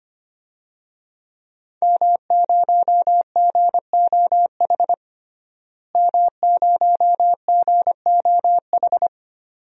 Morse code, heard as a keyed tone, about 25 wpm, 700 Hz. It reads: M0GOH M0GOH
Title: DXCC-100 Call Signs - 5 Characters - Repeated 2x - 25wpm